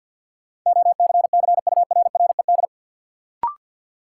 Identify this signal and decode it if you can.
Morse code, heard as a keyed tone, about 53 wpm, 700 Hz. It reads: OCCURRED
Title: Sets of 5 Words Encore - New Word Drill - 50wpm